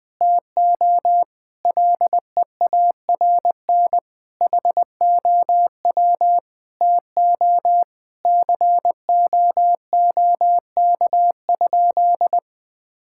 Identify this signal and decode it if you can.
Morse code, heard as a keyed tone, about 20 wpm, 700 Hz. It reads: TO LEARN HOW TO COOK?
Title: Sentences with 7-Word Phrases - 20wpm